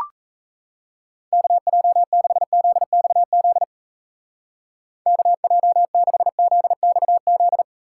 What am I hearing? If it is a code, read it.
Morse code, heard as a keyed tone, about 35 wpm, 700 Hz. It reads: KJ6ZXZ KJ6ZXZ